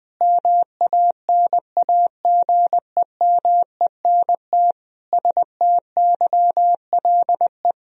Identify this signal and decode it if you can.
Morse code, heard as a keyed tone, about 21 wpm, 700 Hz. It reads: MANAGEMENT STYLE
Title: Sentences from Top 1000 Words - Review Part 4 - 20wpm